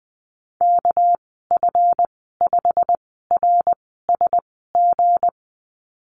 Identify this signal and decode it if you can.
Morse code, heard as a keyed tone, about 20 wpm, spaced as for 14 wpm, 700 Hz. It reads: KF5RSG
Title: 238 Lesson 38g - Letter q - Call Signs 20wpm